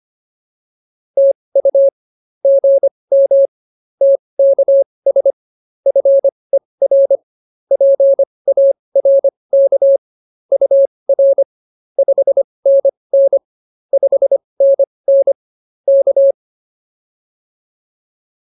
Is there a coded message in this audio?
Yes: Morse code from 1.2 to 16.3 s, heard as a keyed tone, about 24 wpm, spaced as for 20 wpm, 550 Hz. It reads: TU GM TKS FER PARK UR 5NN 5NN K